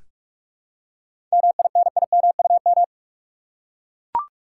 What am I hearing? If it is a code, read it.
Morse code, heard as a keyed tone, about 41 wpm, 700 Hz. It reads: MINIMUM
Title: Seven Letter Words - 45wpm